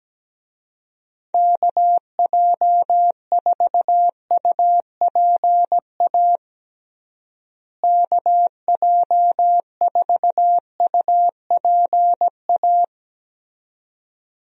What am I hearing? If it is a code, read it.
Morse code, heard as a keyed tone, about 17 wpm, 700 Hz. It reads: KJ4UPA KJ4UPA
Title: USA Call Signs - 6 Characters - Repeated 2x - 17wpm